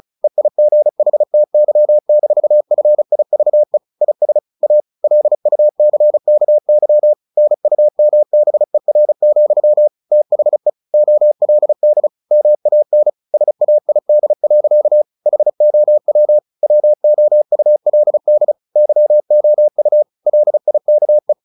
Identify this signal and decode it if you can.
Morse code, heard as a keyed tone, about 34 wpm, 600 Hz. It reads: EIGHTY-FIVE IS A LUCKY NUMBER, THE OLD MAN SAID. HOW WOULD YOU LIKE
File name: the old man and the sea / the old man and the sea 1 35wpm